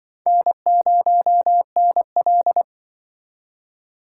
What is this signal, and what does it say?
Morse code, heard as a keyed tone, about 24 wpm, 700 Hz. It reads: N0NL
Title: Speed Racing - USA 1x2 Call Signs 20wpm